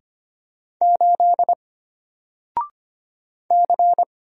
Morse code, heard as a keyed tone, about 25 wpm, 700 Hz. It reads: 8 C
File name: Single-Letter-Number-25wpm